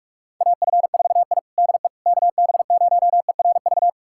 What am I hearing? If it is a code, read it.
Morse code, heard as a keyed tone, about 46 wpm, 690 Hz. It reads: AF4I DE KB0ERV
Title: USA Callsign DE Callsign 45wpm